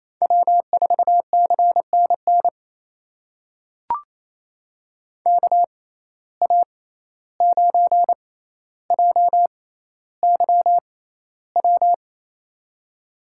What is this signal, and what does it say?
Morse code, heard as a keyed tone, about 28 wpm, spaced as for 14 wpm, 700 Hz. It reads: W4CNN KA9JYW
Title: USA Call Signs - 6x Character Spacing - 28wpm